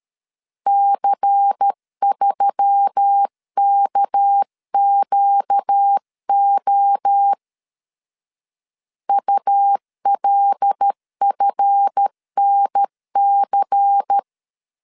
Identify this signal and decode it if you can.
Morse code, heard as a keyed tone, about 13 wpm, 800 Hz. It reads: C3KQO ULFNC
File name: LEZIONI / 7 GF9Z / 7 GF 12wpm 2122